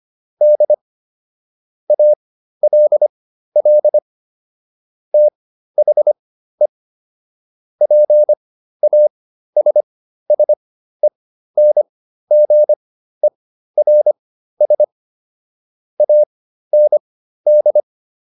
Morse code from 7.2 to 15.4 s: PASSENGERS

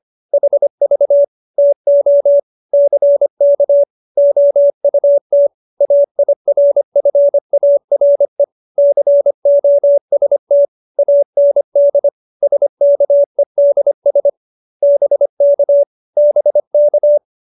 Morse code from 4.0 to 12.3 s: OUT AIRFARE COST AND

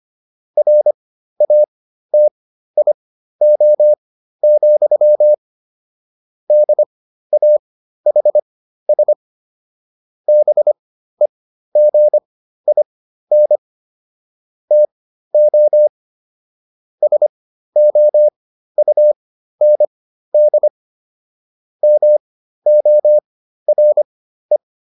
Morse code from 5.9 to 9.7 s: DAHS

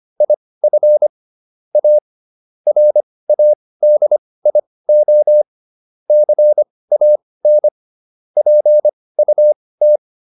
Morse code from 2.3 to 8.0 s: RADIO CAN

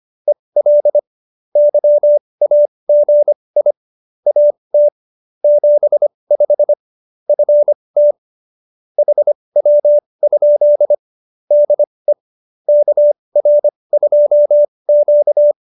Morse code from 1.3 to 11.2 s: YAGI AT 75 FT HW?

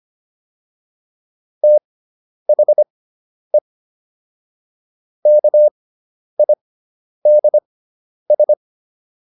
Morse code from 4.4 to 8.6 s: KIDS

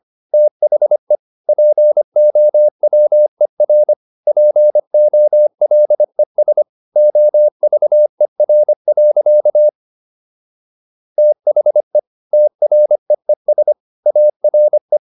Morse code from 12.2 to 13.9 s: TREES

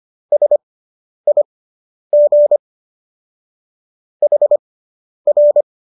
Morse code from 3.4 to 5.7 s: HR

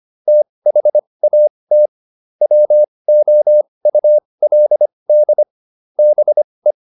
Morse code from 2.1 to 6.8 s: WOULD BE

